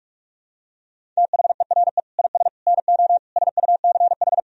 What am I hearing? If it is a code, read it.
Morse code, heard as a keyed tone, about 45 wpm, 700 Hz. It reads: THERE IS NO SUCH
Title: Sentences from Top 1000 Words - Review Part 1 - 45wpm